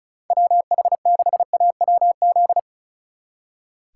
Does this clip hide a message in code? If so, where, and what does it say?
Yes: Morse code from 0.3 to 2.6 s, heard as a keyed tone, about 44 wpm, 700 Hz. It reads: WH6AWZ